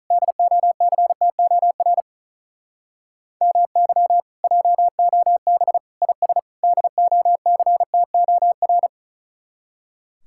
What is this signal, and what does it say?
Morse code, heard as a keyed tone, about 44 wpm, 700 Hz. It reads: DOCTOR MY JOB IS DOCTOR